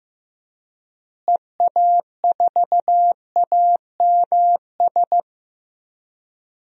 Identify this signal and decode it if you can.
Morse code, heard as a keyed tone, about 15 wpm, 700 Hz. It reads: EA4AMS
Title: DXCC-100 Call Signs - 6 Characters - Repeated 3x - 15wpm